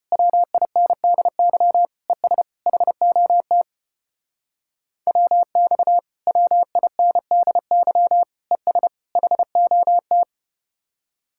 Morse code, heard as a keyed tone, about 33 wpm, 700 Hz. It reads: WINDY ES HOT WX WINDY ES HOT